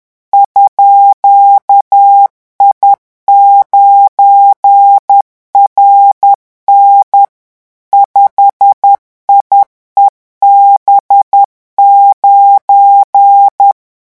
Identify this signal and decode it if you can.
Morse code, heard as a keyed tone, about 10 wpm, 800 Hz. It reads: _I9RN 5IEB9
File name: morse mp3 / revision17-vitesse 10 mots